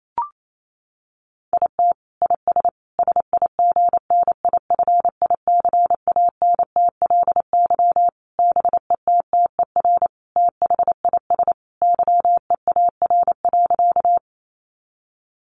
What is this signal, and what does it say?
Morse code, heard as a keyed tone, about 28 wpm, 700 Hz. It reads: IT IS SIGNIFICANTLY BETTER THIS YEAR.
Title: Sentences from Top 1500 Words 28wpm